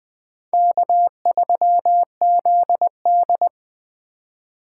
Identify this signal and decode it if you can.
Morse code, heard as a keyed tone, about 19 wpm, 700 Hz. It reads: K3ZD